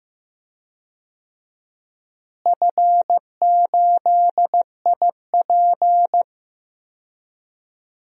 Morse code, heard as a keyed tone, about 15 wpm, 700 Hz. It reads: F8IP
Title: DXCC-100 Call Signs - 4 Characters - Repeated 2x - 15wpm